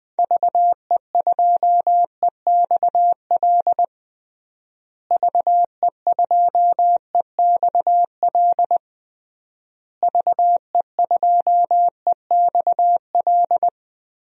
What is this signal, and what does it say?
Morse code, heard as a keyed tone, about 20 wpm, 700 Hz. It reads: VE2EXL VE2EXL VE2EXL
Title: DXCC-100 Call Signs - 6 Characters - Repeated 3x - 20wpm